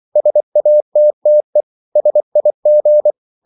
Morse code, heard as a keyed tone, about 24 wpm, 600 Hz. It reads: SATTE SIG
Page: Ferie 24 wpm